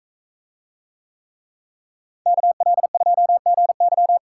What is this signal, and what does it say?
Morse code, heard as a keyed tone, about 40 wpm, 700 Hz. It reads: KL2GY